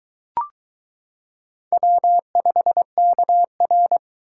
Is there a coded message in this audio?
Yes: Morse code from 1.7 to 4.0 s, heard as a keyed tone, about 22 wpm, 700 Hz. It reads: W5KR